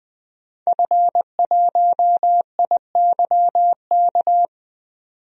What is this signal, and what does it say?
Morse code, heard as a keyed tone, about 20 wpm, 700 Hz. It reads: F1IYK
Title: DXCC-100 Call Signs - 5 Characters - 20wpm